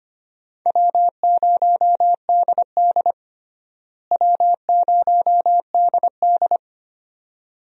Morse code, heard as a keyed tone, about 25 wpm, 700 Hz. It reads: W0DD W0DD
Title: USA Call Signs - 4 Characters - Repeated 2x - 25wpm